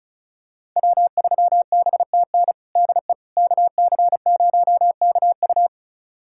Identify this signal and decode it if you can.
Morse code, heard as a keyed tone, about 36 wpm, 700 Hz. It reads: W3BTN DE KC0KU
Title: USA Callsign DE Callsign - 5 Characters - 35wpm